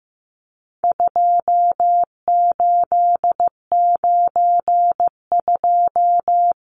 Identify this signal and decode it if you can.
Morse code, heard as a keyed tone, about 15 wpm, 700 Hz. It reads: 2892
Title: Numbers - 4-Digit 15wpm